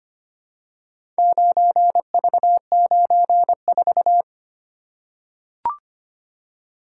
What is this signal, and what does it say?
Morse code, heard as a keyed tone, about 25 wpm, 700 Hz. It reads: 9V94
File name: practice / 25wpm / Four Letter-Number Combinations - 25wpm